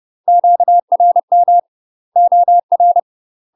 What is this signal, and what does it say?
Morse code, heard as a keyed tone, about 30 wpm, 700 Hz. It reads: QRM OR
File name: UK MASTER QSOs TRAINING / 30WPM / UK MASTER QSOs 0001